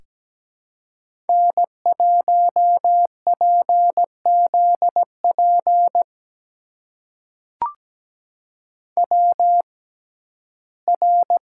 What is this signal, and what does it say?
Morse code, heard as a keyed tone, about 17 wpm, spaced as for 12 wpm, 700 Hz. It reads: N1PZP WR
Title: USA Call Signs - 5 Characters - 6x Character Spacing - 17wpm